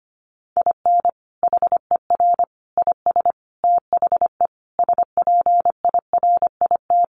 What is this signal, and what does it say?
Morse code, heard as a keyed tone, about 25 wpm, 700 Hz. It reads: IN HER IS THE SPIRIT
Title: Sentences from Top 1400 Words 25wpm